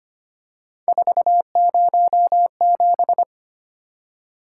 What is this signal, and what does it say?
Morse code, heard as a keyed tone, about 25 wpm, 700 Hz. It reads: 407
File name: Three Letter-Number Combinations - 25wpm